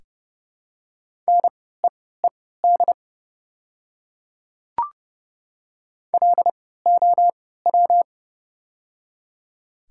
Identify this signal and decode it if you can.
Morse code, heard as a keyed tone, about 30 wpm, spaced as for 17 wpm, 700 Hz. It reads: NEED LOW
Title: 3x Character Spacing - Top 200 Words - 30wpm